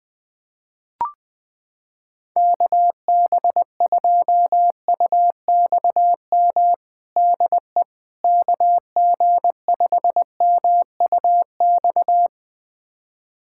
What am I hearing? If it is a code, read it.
Morse code, heard as a keyed tone, about 20 wpm, 700 Hz. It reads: KB2UXM DE KG5MUX
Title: USA Callsign DE Callsign - 6 Characters - 20wpm